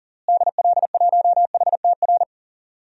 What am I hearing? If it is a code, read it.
Morse code, heard as a keyed tone, about 39 wpm, 700 Hz. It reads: DL1HTR
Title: DXCC-100 Call Signs - 6 Characters - Repeated 2x - 40wpm